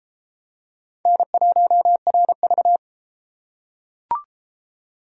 Morse code, heard as a keyed tone, about 31 wpm, 700 Hz. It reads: N1RV